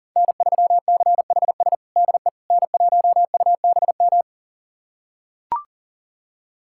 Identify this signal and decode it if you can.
Morse code, heard as a keyed tone, about 40 wpm, 700 Hz. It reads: N3CHS DE N1UBM
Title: USA Callsign DE Callsign - 5 Characters - 40wpm